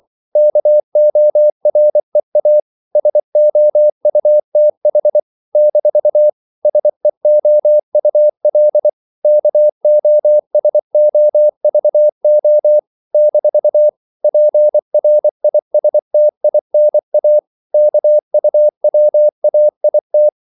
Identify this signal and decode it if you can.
Morse code, heard as a keyed tone, about 24 wpm, 600 Hz. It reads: KOREA SOUTH - SEOUL KOSOVO - PRISTINA KUWAIT
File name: CW / Practice / NationalCapitals24